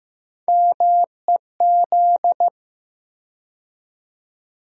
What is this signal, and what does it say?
Morse code, heard as a keyed tone, about 15 wpm, 700 Hz. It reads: MEZ
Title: USA Call Sign Postfixes - 3 Letters 15wpm